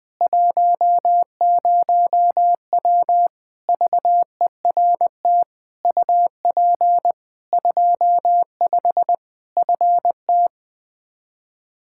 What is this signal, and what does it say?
Morse code, heard as a keyed tone, about 20 wpm, 700 Hz. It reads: 10W VERT UP 25 FT
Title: Practice QSO - Exchange 2 - Info - 20wpm